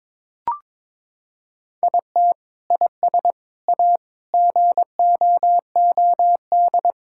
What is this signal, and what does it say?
Morse code, heard as a keyed tone, about 22 wpm, 700 Hz. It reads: IT IS A GOOD
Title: Sentences from Top 1000 Words - Review Part 6 - 22wpm